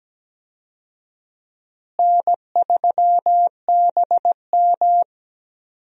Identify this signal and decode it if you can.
Morse code, heard as a keyed tone, about 17 wpm, 700 Hz. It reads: N3BM
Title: USA Call Signs - 4 Characters - Repeated 2x - 17wpm